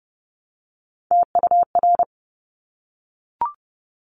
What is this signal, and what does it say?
Morse code, heard as a keyed tone, about 30 wpm, 700 Hz. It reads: TUR
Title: Top Three Letter Combinations in Words 30wpm